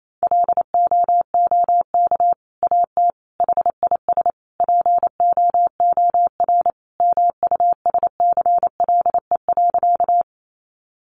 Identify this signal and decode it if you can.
Morse code, heard as a keyed tone, about 28 wpm, 700 Hz. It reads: LOOK AT HIS POOR MUSCLE.
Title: Sentences from Top 1300 Words 28wpm